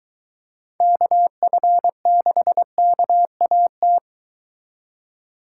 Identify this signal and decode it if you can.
Morse code, heard as a keyed tone, about 23 wpm, 700 Hz. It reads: KF6KAT